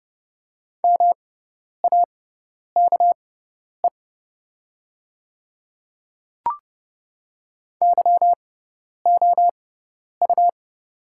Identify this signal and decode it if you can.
Morse code, heard as a keyed tone, about 30 wpm, spaced as for 10 wpm, 700 Hz. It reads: MAKE YOU
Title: 6x Character Spacing - Top 100 Words - 30wpm